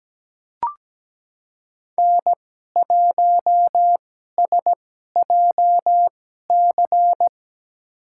Morse code, heard as a keyed tone, about 17 wpm, spaced as for 12 wpm, 700 Hz. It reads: N1SJC
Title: USA Call Signs - 2x Character Spacing - 17wpm